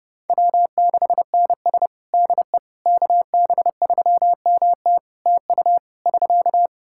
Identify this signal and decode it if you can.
Morse code, heard as a keyed tone, about 30 wpm, 700 Hz. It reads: W6NS DE KB3MT TU <SK>
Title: Practice QSO - Exchange 4 - Ending - 30wpm